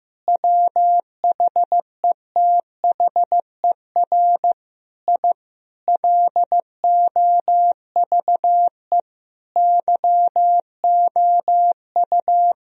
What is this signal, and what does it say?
Morse code, heard as a keyed tone, about 15 wpm, 700 Hz. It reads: WHETHER I LOVE YOU
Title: Sentences from Top 1000 Words - Review Part 3 - 15wpm